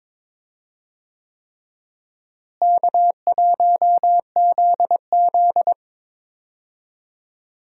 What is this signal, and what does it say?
Morse code, heard as a keyed tone, about 22 wpm, 700 Hz. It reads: K1ZZ
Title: USA Call Signs - 4 Characters - Repeated 3x - 22wpm